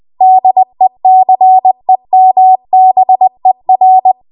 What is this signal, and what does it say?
Morse code, heard as a keyed tone, about 20 wpm, 750 Hz. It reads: DECEMBER
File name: Archive / 20 WPM / 20WPM